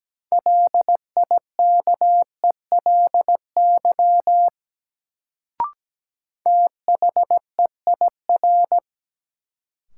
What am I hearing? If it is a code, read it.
Morse code, heard as a keyed tone, about 17 wpm, 700 Hz. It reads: LIKELY THEIR
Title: Sets of 2 Words - New Word Drill - 17wpm